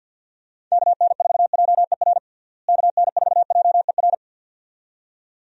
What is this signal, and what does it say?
Morse code, heard as a keyed tone, about 51 wpm, 700 Hz. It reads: KN4JER KN4JER
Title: USA Call Signs - 6 Characters - Repeated 2x - 50wpm